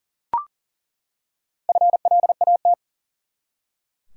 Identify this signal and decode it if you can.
Morse code, heard as a keyed tone, about 40 wpm, 700 Hz. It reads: FLAT
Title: Four Letter Words - 40wpm